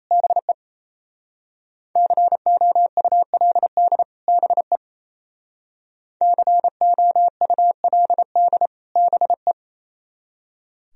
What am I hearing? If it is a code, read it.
Morse code, heard as a keyed tone, about 34 wpm, 700 Hz. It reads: BE COULD BE COULD BE
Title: Speed-Racing - Sets of 2 Words 28wpm